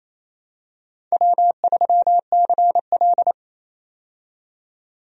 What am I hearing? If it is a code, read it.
Morse code, heard as a keyed tone, about 27 wpm, 700 Hz. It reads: W3CL
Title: USA Call Signs - 4 Characters - 28wpm